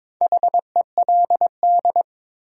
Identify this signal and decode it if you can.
Morse code, heard as a keyed tone, about 22 wpm, 700 Hz. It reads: HELD
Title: Sentences from Top 1600 Words - New Word Drill - 22wpm